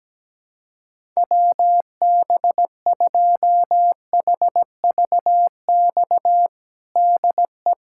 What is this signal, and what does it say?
Morse code, heard as a keyed tone, about 17 wpm, 700 Hz. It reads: WB2HVX DE
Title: USA Callsign DE Callsign - 6 Characters - 17wpm